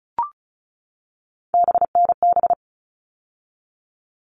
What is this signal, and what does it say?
Morse code, heard as a keyed tone, about 34 wpm, 700 Hz. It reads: BNB
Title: All Cut Numbers - 3-Digit 35wpm